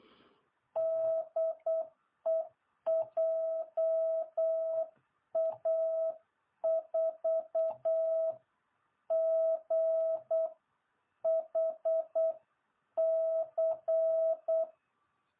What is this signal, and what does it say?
Morse code, heard as a keyed tone, about 8 wpm, spaced as for 7 wpm, 660 Hz. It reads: DE JA4GHC